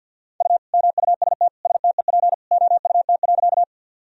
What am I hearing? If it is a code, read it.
Morse code, heard as a keyed tone, about 46 wpm, 700 Hz. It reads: U MUST STEP OUT.
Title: Sentences from Top 400 Words and Abbreviations 50wpm